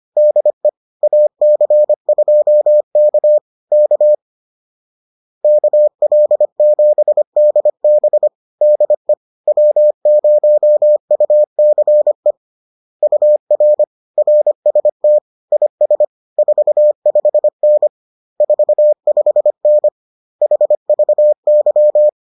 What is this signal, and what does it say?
Morse code, heard as a keyed tone, about 25 wpm, 600 Hz. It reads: DE AC2K K KL7DB DE W0UCE UR RST IS 45N 45N HVY